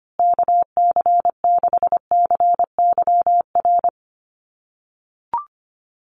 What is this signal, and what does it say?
Morse code, heard as a keyed tone, about 25 wpm, 700 Hz. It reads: KC6CYR
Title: USA Call Signs Repeated 2x 25wpm